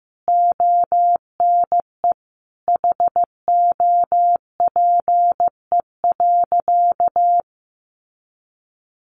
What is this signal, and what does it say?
Morse code, heard as a keyed tone, about 15 wpm, 700 Hz. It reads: ONE HOPE.
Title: Sentences from Top 1000 Words 15wpm